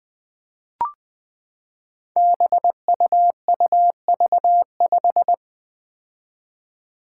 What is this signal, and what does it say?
Morse code, heard as a keyed tone, about 20 wpm, 700 Hz. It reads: BUUV5
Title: Five Challenging Letter-Number Combinations 20wpm